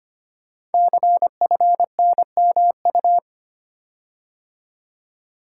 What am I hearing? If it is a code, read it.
Morse code, heard as a keyed tone, about 25 wpm, 700 Hz. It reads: CFNMU
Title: Code Groups - 5 Letters - 25wpm